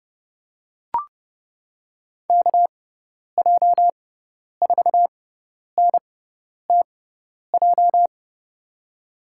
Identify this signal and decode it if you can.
Morse code, heard as a keyed tone, about 30 wpm, spaced as for 10 wpm, 700 Hz. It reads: KJ4NTJ
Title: USA Call Signs - 6 Characters - 6x Character Spacing - 30wpm